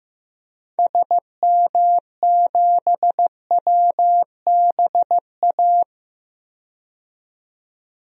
Morse code, heard as a keyed tone, about 16 wpm, 700 Hz. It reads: SM7WBA
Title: DXCC-100 Call Signs - 6 Characters - 15wpm